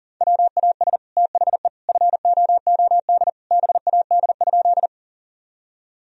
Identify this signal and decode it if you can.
Morse code, heard as a keyed tone, about 40 wpm, 700 Hz. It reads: WAS THE FOOD BAD?
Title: Sentences from Top 1000 Words - Review Part 1 - 40wpm